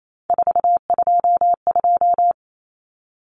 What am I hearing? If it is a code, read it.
Morse code, heard as a keyed tone, about 28 wpm, 700 Hz. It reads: 422
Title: Numbers - 3-Digit 28wpm